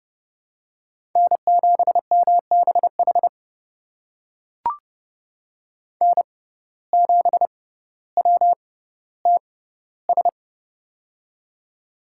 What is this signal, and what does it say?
Morse code, heard as a keyed tone, about 30 wpm, spaced as for 15 wpm, 700 Hz. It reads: N7MBH N7WTS
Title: USA Call Signs - 5 Characters - 6x Character Spacing - 30wpm